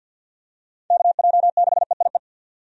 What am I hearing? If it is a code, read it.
Morse code, heard as a keyed tone, about 50 wpm, 700 Hz. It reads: KJ6EIE
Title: USA Call Signs - 6 Characters - 50wpm